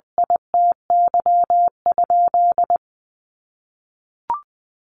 Morse code, heard as a keyed tone, about 19 wpm, 700 Hz. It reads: ITY?
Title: Sentences from Top 1200 Words 20wpm